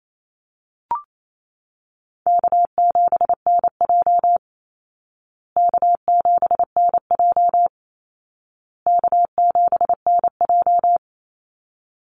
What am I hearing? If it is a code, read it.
Morse code, heard as a keyed tone, about 28 wpm, 700 Hz. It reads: K7NJ K7NJ K7NJ
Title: Call Signs Repeated 2x - 28wpm